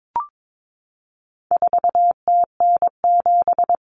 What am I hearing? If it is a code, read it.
Morse code, heard as a keyed tone, about 22 wpm, 700 Hz. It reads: 4TN7
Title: Common Cut Numbers - 4-Digit 22wpm